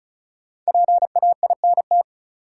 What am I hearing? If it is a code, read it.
Morse code, heard as a keyed tone, about 36 wpm, 700 Hz. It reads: PAINT
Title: Sentences from Top 1000 Words - Review Part 6 - New Word Drill - 35wpm